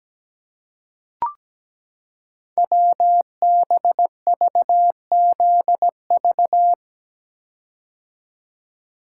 Morse code, heard as a keyed tone, about 17 wpm, 700 Hz. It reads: WBVZV
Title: Code Groups - 5 Letters - 17wpm